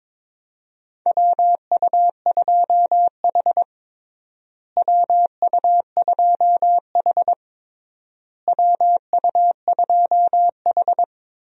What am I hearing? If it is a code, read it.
Morse code, heard as a keyed tone, about 22 wpm, 700 Hz. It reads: WU2H WU2H WU2H
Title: USA Call Signs - 4 Characters - Repeated 3x - 22wpm